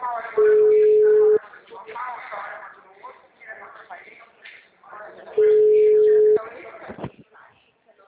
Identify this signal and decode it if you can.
Ringback tone (Europe, 425 Hz), with speech.
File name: out- -2012- - - .64